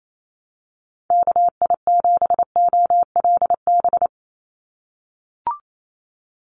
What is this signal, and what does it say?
Morse code, heard as a keyed tone, about 28 wpm, 700 Hz. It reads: KI7OLB
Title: USA Call Signs 28wpm